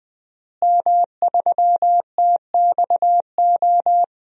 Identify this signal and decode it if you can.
Morse code, heard as a keyed tone, about 21 wpm, 700 Hz. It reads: M3TXO
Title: DXCC-100 Call Signs - 5 Characters - Repeated 2x - 20wpm